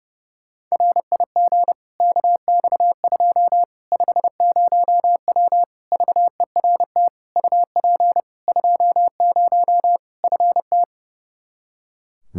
Morse code, heard as a keyed tone, about 31 wpm, 700 Hz. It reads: RIG KX2 50W VERT UP 20 FT
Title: Practice QSO - Exchange 2 - Info - 30wpm